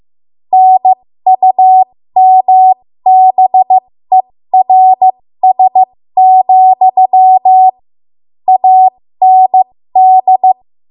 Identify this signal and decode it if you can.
Morse code, heard as a keyed tone, about 15 wpm, spaced as for 13 wpm, 750 Hz. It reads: NUMBERS, AND
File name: Morse / 13WPM